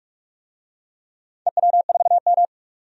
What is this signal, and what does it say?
Morse code, heard as a keyed tone, about 45 wpm, 700 Hz. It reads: EW4M
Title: DXCC-100 Call Signs - 4 Characters - 45wpm